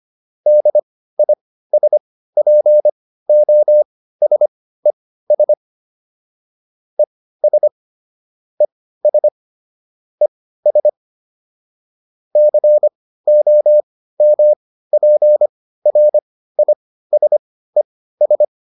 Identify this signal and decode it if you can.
Morse code, heard as a keyed tone, about 25 wpm, spaced as for 15 wpm, 600 Hz. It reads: DISPOSES ES ES ES COMPRISES